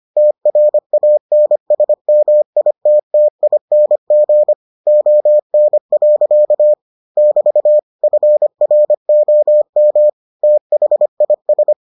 Morse code, heard as a keyed tone, about 25 wpm, 600 Hz. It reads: TRANSMITTING ON. <BT> FROM THIS